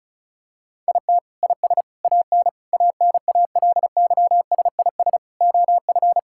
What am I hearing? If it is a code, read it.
Morse code, heard as a keyed tone, about 34 wpm, 700 Hz. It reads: IT IS AN ANALYSIS OF
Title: Sentences from Top 1000 Words - Review Part 5 - 35wpm